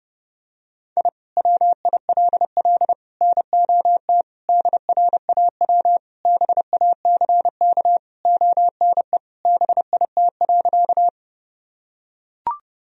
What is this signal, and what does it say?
Morse code, heard as a keyed tone, about 31 wpm, 700 Hz. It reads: I WILL NOT DRAW BACK ONE BIT.
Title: Sentences from Top 1000 Words - Review Part 1 - 30wpm